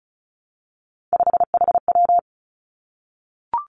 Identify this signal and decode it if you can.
Morse code, heard as a keyed tone, about 36 wpm, 700 Hz. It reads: 5HW
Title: Three Challenging Letter-Number Combinations 35wpm